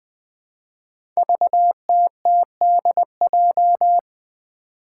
Morse code, heard as a keyed tone, about 20 wpm, 700 Hz. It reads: VTTDJ
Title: Five Challenging Letter-Number Combinations 20wpm